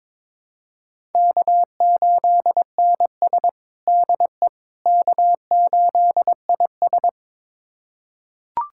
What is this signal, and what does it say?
Morse code, heard as a keyed tone, about 22 wpm, 700 Hz. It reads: K8NS DE K8IS
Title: USA Callsign DE Callsign - 4 Characters - 22wpm